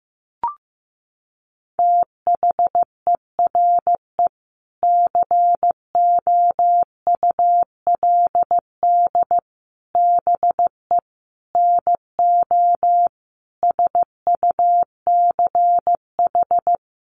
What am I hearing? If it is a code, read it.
Morse code, heard as a keyed tone, about 15 wpm, 700 Hz. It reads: THERE COULD BE NO SUCH
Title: Sentences from Top 600 Words 15wpm